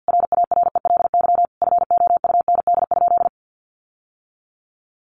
Morse code, heard as a keyed tone, about 50 wpm, 700 Hz. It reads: RARELY FOUND?
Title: Sentences from Top 1400 Words 50wpm